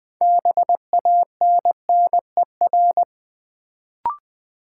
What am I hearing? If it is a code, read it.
Morse code, heard as a keyed tone, about 20 wpm, 700 Hz. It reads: BANNER
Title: Build A Word - 3 Letters At A Time - 6-Letter Words - 20wpm